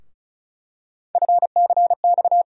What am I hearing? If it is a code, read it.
Morse code, heard as a keyed tone, about 36 wpm, 700 Hz. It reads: FCX
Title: USA Call Sign Postfixes - 3 Letters 35wpm